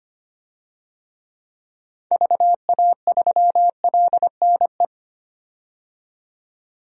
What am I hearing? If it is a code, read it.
Morse code, heard as a keyed tone, about 25 wpm, 700 Hz. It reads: VA3LNE
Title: DXCC-100 Call Signs - 6 Characters - 25wpm